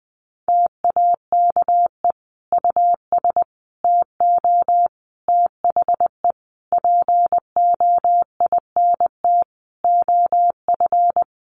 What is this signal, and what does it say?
Morse code, heard as a keyed tone, about 20 wpm, 700 Hz. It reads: TAKE US TO THE POINT OF
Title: Sentences from Top 1500 Words 20wpm